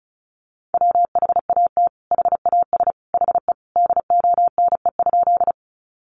Morse code, heard as a keyed tone, about 34 wpm, 700 Hz. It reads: WHAT HAS HE DONE?
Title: Sentences from Top 500 Words 35wpm